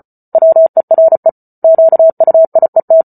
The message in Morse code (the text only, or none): WERE QUIET